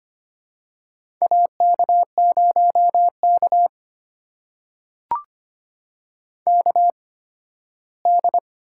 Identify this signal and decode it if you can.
Morse code, heard as a keyed tone, about 25 wpm, spaced as for 15 wpm, 700 Hz. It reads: AK0K KD
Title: USA Call Signs - 8x Character Spacing - 25wpm